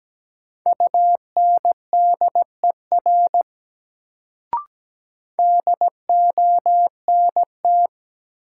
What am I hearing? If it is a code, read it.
Morse code, heard as a keyed tone, about 17 wpm, 700 Hz. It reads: UNDER DONT
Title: Sentences from Top 300 Words - New Word Drill - 17wpm